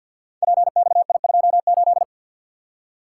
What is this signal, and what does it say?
Morse code, heard as a keyed tone, about 53 wpm, 700 Hz. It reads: PXI28
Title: Code Groups - 5 Letter:Numbers - 50wpm